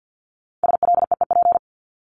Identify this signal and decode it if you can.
Morse code, heard as a keyed tone, about 50 wpm, 700 Hz. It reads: SLEEP
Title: Top 500 Words 50wpm